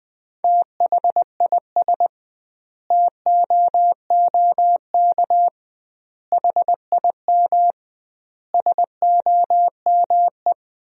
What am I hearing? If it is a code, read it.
Morse code, heard as a keyed tone, about 20 wpm, 700 Hz. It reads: THIS TOOK HIM SOME
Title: 2x Word-Spacing - Sentences from Top 300 Words - 20wpm